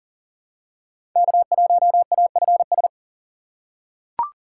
Morse code, heard as a keyed tone, about 40 wpm, 700 Hz. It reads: K1AFS
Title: USA Call Signs - 5 Characters - 4x Character Spacing - 40wpm